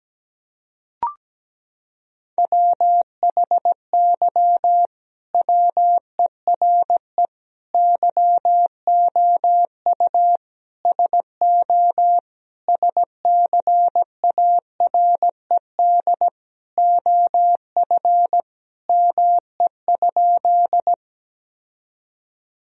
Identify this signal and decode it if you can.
Morse code, heard as a keyed tone, about 17 wpm, 700 Hz. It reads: WHY WERE YOU SO SCARED OF ME?
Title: Sentences from Top 1000 Words - Review Part 4 - 17wpm